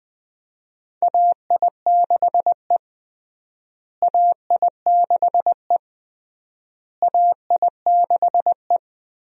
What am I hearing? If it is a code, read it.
Morse code, heard as a keyed tone, about 21 wpm, 700 Hz. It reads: AI6E AI6E AI6E
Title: USA Call Signs - 4 Characters - Repeated 3x - 20wpm